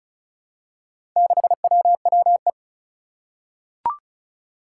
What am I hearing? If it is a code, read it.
Morse code, heard as a keyed tone, about 36 wpm, 700 Hz. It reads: 6WWE
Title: Four Challenging Letter-Number Combinations 35wpm